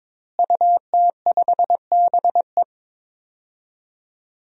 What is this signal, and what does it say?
Morse code, heard as a keyed tone, about 22 wpm, 700 Hz. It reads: UT5BE
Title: DXCC-100 Call Signs - 5 Characters - Repeated 2x - 22wpm